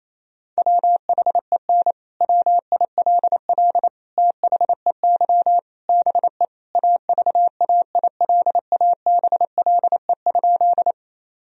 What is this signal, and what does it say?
Morse code, heard as a keyed tone, about 28 wpm, 700 Hz. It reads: WHEN WILL THEY BE AVAILABLE?
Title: Sentences from Top 1000 Words - Review Part 3 - 28wpm